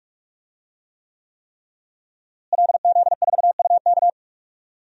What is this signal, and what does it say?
Morse code, heard as a keyed tone, about 46 wpm, 700 Hz. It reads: LZ4UK
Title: DXCC-100 Call Signs - 5 Characters - Repeated 3x - 45wpm